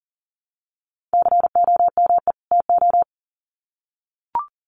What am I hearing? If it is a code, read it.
Morse code, heard as a keyed tone, about 40 wpm, 700 Hz. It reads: COME TO
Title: Sets of 2 Words 40wpm